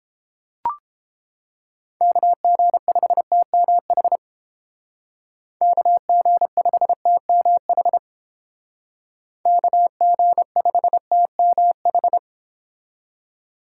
Morse code, heard as a keyed tone, about 29 wpm, 700 Hz. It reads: KG5TMH KG5TMH KG5TMH